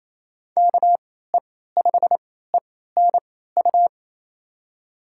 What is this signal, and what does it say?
Morse code, heard as a keyed tone, about 28 wpm, spaced as for 16 wpm, 700 Hz. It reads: KE5ENU
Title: USA Call Signs - 6 Characters - 3x Character Spacing - 28wpm